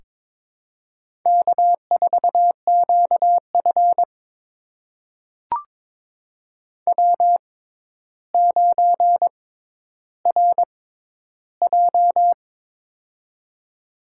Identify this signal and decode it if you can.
Morse code, heard as a keyed tone, about 22 wpm, spaced as for 11 wpm, 700 Hz. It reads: K4QF W9RJ
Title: USA Call Signs - 4 Characters - 6x Character Spacing - 22wpm